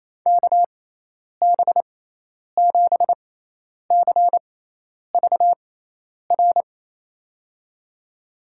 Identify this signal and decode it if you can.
Morse code, heard as a keyed tone, about 28 wpm, spaced as for 10 wpm, 700 Hz. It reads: KB7CVR
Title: USA Call Signs - 6 Characters - 6x Character Spacing - 28wpm